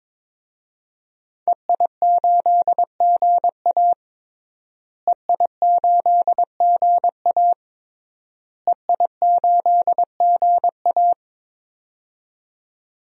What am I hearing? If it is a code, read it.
Morse code, heard as a keyed tone, about 22 wpm, 700 Hz. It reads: EI8GA EI8GA EI8GA
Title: DXCC-100 Call Signs - 5 Characters - Repeated 3x - 22wpm